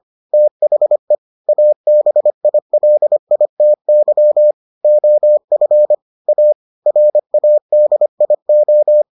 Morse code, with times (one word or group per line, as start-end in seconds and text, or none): 0.3-1.1 THE
1.5-4.5 ABILITY
4.8-5.9 OF
6.3-6.5 A
6.9-9.0 RADIO